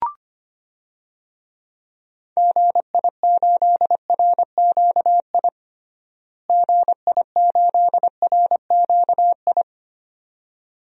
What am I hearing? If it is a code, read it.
Morse code, heard as a keyed tone, about 25 wpm, 700 Hz. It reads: GI8RQI GI8RQI
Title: DXCC-100 Call Signs - 6 Characters - Repeated 2x - 25wpm